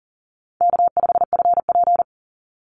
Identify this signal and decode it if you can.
Morse code, heard as a keyed tone, about 40 wpm, 700 Hz. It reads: K5FP
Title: Call Signs - 40wpm